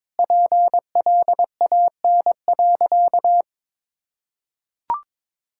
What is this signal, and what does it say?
Morse code, heard as a keyed tone, about 21 wpm, 700 Hz. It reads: PLAN.
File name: practice / 22wpm / Sentences with 3-Word Phrases - 22wpm